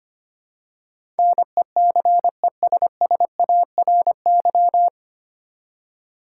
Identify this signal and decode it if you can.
Morse code, heard as a keyed tone, about 25 wpm, 700 Hz. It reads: NECESSARY
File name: Nine Letter Words - 25wpm